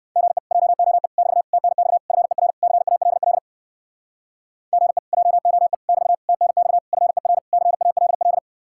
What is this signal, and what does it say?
Morse code, heard as a keyed tone, about 58 wpm, 700 Hz. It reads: GE JOE <BT> TNX FER CALL GE JOE <BT> TNX FER CALL